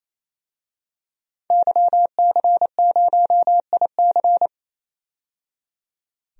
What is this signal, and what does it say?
Morse code, heard as a keyed tone, about 28 wpm, 700 Hz. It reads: YC0IC
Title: DXCC-100 Call Signs - 5 Characters - 28wpm